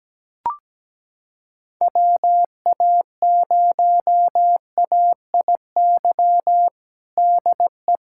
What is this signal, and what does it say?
Morse code, heard as a keyed tone, about 17 wpm, 700 Hz. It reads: WA0AIY DE
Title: USA Callsign DE Callsign 17wpm